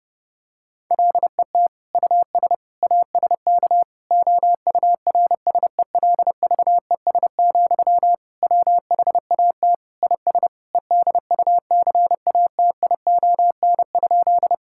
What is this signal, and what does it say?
Morse code, heard as a keyed tone, about 32 wpm, 700 Hz. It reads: LET US ASK OURSELVES, WHAT IS EDUCATION?
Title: Sentences from Top 1600 Words 30wpm